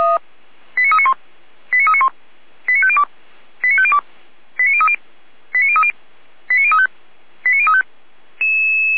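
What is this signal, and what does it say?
Five-tone selective call (ZVEI), heard as quick runs of five tones, tones of about 70 ms. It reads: 89391 89391 89502 89502 89030 89030 89035 89035
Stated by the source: Received on SDR.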